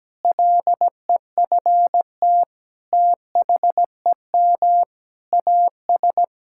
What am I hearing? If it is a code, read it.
Morse code, heard as a keyed tone, about 17 wpm, 700 Hz. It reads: LEFT THEM AS
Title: Sentences from Top 1000 Words - Review Part 5 - 17wpm